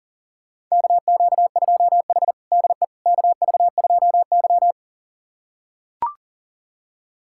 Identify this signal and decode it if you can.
Morse code, heard as a keyed tone, about 40 wpm, 700 Hz. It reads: KQ2H DE KV2Y
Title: USA Callsign DE Callsign - 4 Characters - 40wpm